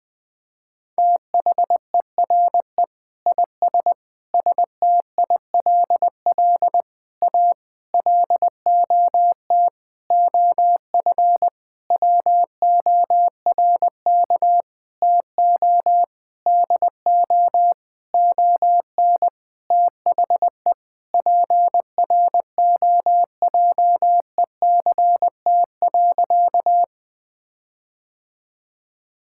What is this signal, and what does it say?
Morse code, heard as a keyed tone, about 20 wpm, 700 Hz. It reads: THERE IS STILL A LOT OF WORK TO DO ON THE PROJECT.
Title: Sentences with 9-Word Phrases - 20wpm